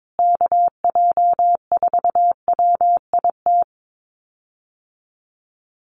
Morse code, heard as a keyed tone, about 22 wpm, 700 Hz. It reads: KJ4WIT
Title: USA Call Signs Repeated 3x 22wpm